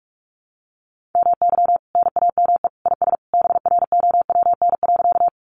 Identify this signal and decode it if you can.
Morse code, heard as a keyed tone, about 44 wpm, 700 Hz. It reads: MY NAME IS BROWN.
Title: Sentences from Top 1200 Words 45wpm